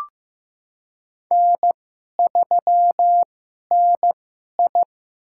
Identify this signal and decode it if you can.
Morse code, heard as a keyed tone, about 15 wpm, spaced as for 11 wpm, 690 Hz. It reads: N3NI
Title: USA Call Signs - 4 Characters - 2x Character Spacing - 15wpm